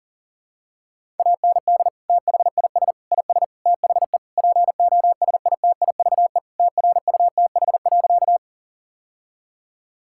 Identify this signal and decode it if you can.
Morse code, heard as a keyed tone, about 40 wpm, 700 Hz. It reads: AND THIS IS THE POSITIVE TRUTH.
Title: Sentences from Top 1000 Words - Review Part 5 - 40wpm